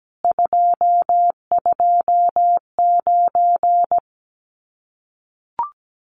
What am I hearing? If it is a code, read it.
Morse code, heard as a keyed tone, about 17 wpm, 700 Hz. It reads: 229
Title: Common QSO Phrases Encore 17wpm